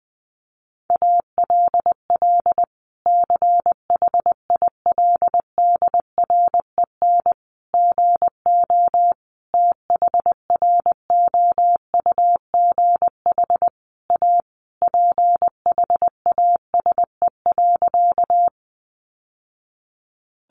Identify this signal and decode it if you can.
Morse code, heard as a keyed tone, about 20 wpm, 700 Hz. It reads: ALL CHILDREN GO THROUGH A PHASE.
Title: Sentences from Top 1200 Words 20wpm